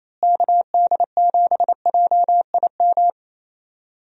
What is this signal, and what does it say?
Morse code, heard as a keyed tone, about 28 wpm, 700 Hz. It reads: KD7JIM
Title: USA Call Signs - 6 Characters - Repeated 3x - 28wpm